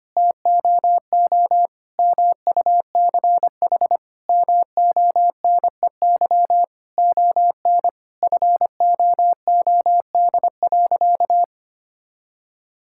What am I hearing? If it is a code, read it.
Morse code, heard as a keyed tone, about 25 wpm, 700 Hz. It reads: TOO MUCH MONEY ON FOOD.
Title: Sentences from Top 1000 Words - Review Part 5 - 25wpm